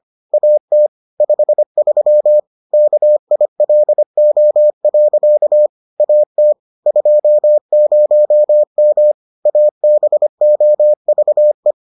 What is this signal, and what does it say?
Morse code, heard as a keyed tone, about 25 wpm, 600 Hz. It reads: AT 53 KILO. AT 20M ABOVE